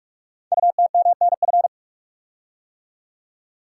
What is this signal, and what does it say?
Morse code, heard as a keyed tone, about 44 wpm, 690 Hz. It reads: UTMNF
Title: Code Groups - 5 Letters - 45wpm